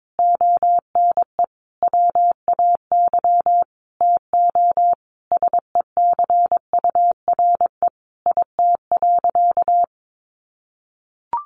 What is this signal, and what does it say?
Morse code, heard as a keyed tone, about 22 wpm, 700 Hz. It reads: ONE WAY TO SECURE IT.
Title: Sentences from Top 1200 Words 22wpm